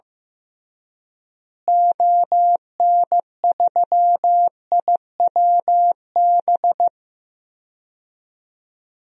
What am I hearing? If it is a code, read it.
Morse code, heard as a keyed tone, about 15 wpm, 700 Hz. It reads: ON3IWB
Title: DXCC-100 Call Signs - 6 Characters - Repeated 3x - 15wpm